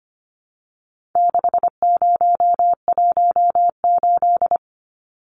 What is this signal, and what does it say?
Morse code, heard as a keyed tone, about 25 wpm, 700 Hz. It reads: 6018
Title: Numbers - 4-Digit 25wpm